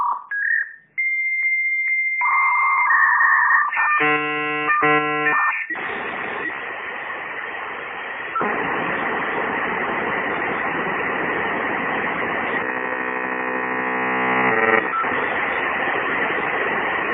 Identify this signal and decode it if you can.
Dial-up modem handshake.